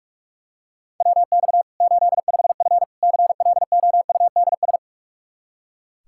Morse code, heard as a keyed tone, about 46 wpm, 700 Hz. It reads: WX 85F CLOUDS
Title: Practice QSO - Exchange 2 - Info - 45wpm